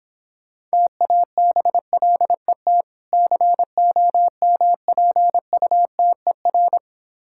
Morse code, heard as a keyed tone, about 26 wpm, 700 Hz. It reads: TABLET COMPUTER